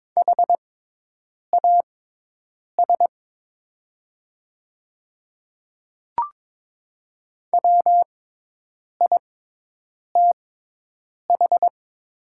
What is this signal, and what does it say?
Morse code, heard as a keyed tone, about 22 wpm, spaced as for 8 wpm, 700 Hz. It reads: HAS WITH
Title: 6x Character Spacing - Top 100 Words - 22wpm